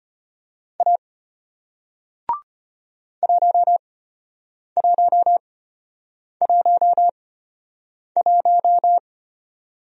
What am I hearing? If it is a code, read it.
Morse code, heard as a keyed tone, about 33 wpm, spaced as for 8 wpm, 700 Hz. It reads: A 1111